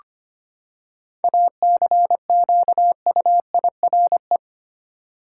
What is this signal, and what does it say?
Morse code, heard as a keyed tone, about 25 wpm, 700 Hz. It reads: ACQUIRE
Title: Seven Letter Words - 25wpm